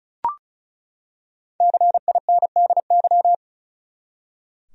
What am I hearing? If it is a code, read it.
Morse code, heard as a keyed tone, about 36 wpm, 700 Hz. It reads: CINDY
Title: USA Common Names - 35wpm